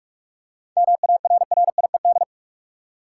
Morse code, heard as a keyed tone, about 46 wpm, 700 Hz. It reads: MARRIED
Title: Seven Letter Words - 45wpm